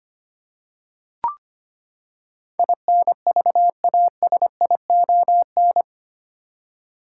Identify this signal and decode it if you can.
Morse code, heard as a keyed tone, about 25 wpm, 700 Hz. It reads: INVASION
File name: Eight Letter Words - 25wpm